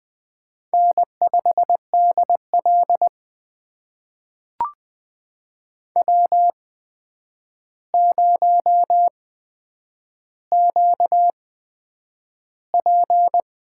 Morse code, heard as a keyed tone, about 20 wpm, spaced as for 9 wpm, 700 Hz. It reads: N5DL W0QP
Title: USA Call Signs - 4 Characters - 8x Character Spacing - 20wpm